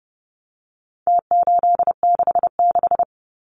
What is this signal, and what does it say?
Morse code, heard as a keyed tone, about 30 wpm, 700 Hz. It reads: T866
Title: Numbers - Contest Style - 4-Digit 30wpm